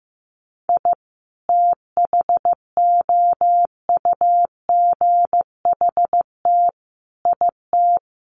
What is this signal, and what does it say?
Morse code, heard as a keyed tone, about 15 wpm, 700 Hz. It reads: I THOUGHT IT
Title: Sets of 5 Words 15wpm